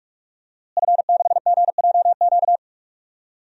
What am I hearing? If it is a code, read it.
Morse code, heard as a keyed tone, about 46 wpm, 700 Hz. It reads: F6GJQ
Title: DXCC-100 Call Signs - 5 Characters - Repeated 2x - 45wpm